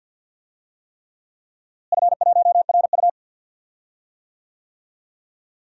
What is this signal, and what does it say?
Morse code, heard as a keyed tone, about 52 wpm, 700 Hz. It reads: F1RU
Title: DXCC-100 Call Signs - 4 Characters - Repeated 3x - 50wpm